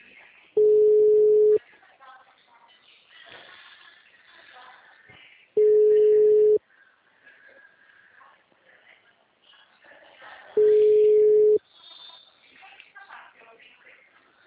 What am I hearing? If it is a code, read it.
Ringback tone (Europe, 425 Hz), with speech.